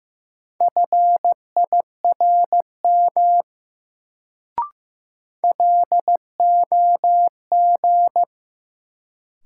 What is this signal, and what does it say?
Morse code, heard as a keyed tone, about 15 wpm, 700 Hz. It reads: FIRM LOG
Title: Sentences from Top 1400 Words - New Word Drill - 15wpm